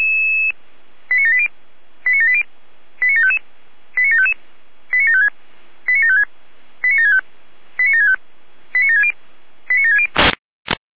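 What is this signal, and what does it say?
Five-tone selective call (ZVEI), heard as quick runs of five tones, tones of about 70 ms. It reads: 89780 89780 89755 89755 89756 89756 89765 89765 89760 89760
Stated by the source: ZVEI five-tone selective call received by SDR.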